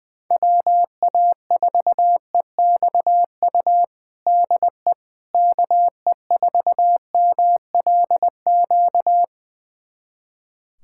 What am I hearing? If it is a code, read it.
Morse code, heard as a keyed tone, about 20 wpm, 700 Hz. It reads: WA4EXU DE KE4MLQ
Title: USA Callsign DE Callsign - 6 Characters - 20wpm